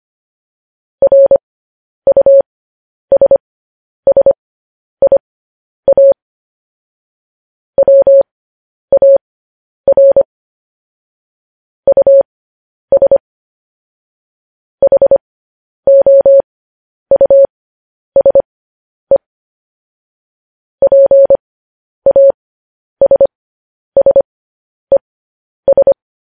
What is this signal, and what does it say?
Morse code, heard as a keyed tone, about 25 wpm, spaced as for 10 wpm, 550 Hz. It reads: RUSSIA WAR US HOUSE PASSES